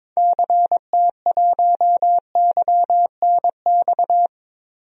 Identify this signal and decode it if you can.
Morse code, heard as a keyed tone, about 22 wpm, 700 Hz. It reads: CT1YNX
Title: DXCC-100 Call Signs - 6 Characters - Repeated 3x - 22wpm